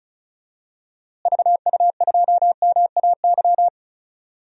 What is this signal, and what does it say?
Morse code, heard as a keyed tone, about 36 wpm, 700 Hz. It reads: VU2MAY
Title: DXCC-100 Call Signs - 6 Characters - Repeated 2x - 35wpm